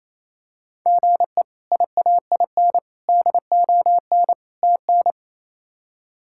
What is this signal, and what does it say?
Morse code, heard as a keyed tone, about 28 wpm, 700 Hz. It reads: GE IAIN DON TN
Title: K1USN SST QSO Practice - 28wpm